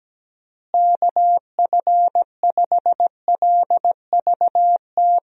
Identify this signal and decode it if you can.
Morse code, heard as a keyed tone, about 17 wpm, 700 Hz. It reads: KF5LVT
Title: USA Call Signs - 6 Characters - Repeated 3x - 17wpm